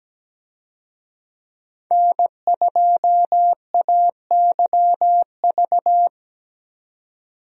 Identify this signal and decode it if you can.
Morse code, heard as a keyed tone, about 17 wpm, 700 Hz. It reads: N2AYV
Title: USA Call Signs - 5 Characters - 17wpm